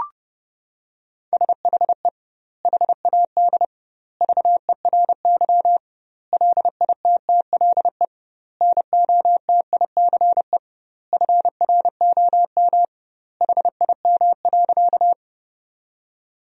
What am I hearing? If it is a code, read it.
Morse code, heard as a keyed tone, about 32 wpm, 700 Hz. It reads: SHE HAD VERY LITTLE NOTICE FROM HIM.
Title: 2x Word-Spacing - Sentences from Top 400 Words - 30wpm